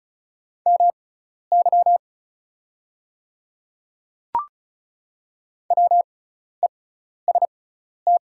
Morse code, heard as a keyed tone, about 36 wpm, spaced as for 12 wpm, 690 Hz. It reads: MY WEST
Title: 6x Character Spacing - Top 500 Words - 35wpm